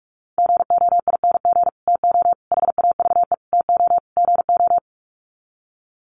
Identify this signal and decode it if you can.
Morse code, heard as a keyed tone, about 44 wpm, 700 Hz. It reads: GOING TO HAVE TO GO
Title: Sets of 5 Words Encore 45wpm